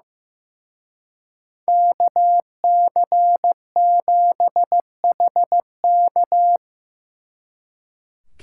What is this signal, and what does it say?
Morse code, heard as a keyed tone, about 18 wpm, 700 Hz. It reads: KC7HK
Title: Speed Racing - USA 2x2 Call Signs 15wpm